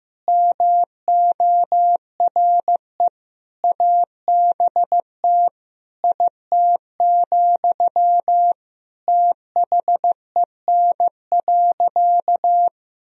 Morse code, heard as a keyed tone, about 15 wpm, 700 Hz. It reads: MORE ABT IT, THEN.
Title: Sentences from Top 200 Words and Abbreviations 15wpm